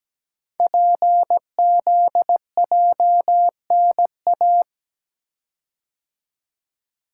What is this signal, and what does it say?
Morse code, heard as a keyed tone, about 17 wpm, 700 Hz. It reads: PZJNA
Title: Code Groups - 5 Letters - 17wpm